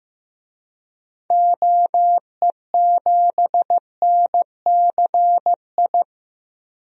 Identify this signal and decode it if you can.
Morse code, heard as a keyed tone, about 15 wpm, 700 Hz. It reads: OE7NCI
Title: DXCC-100 Call Signs - 6 Characters - 15wpm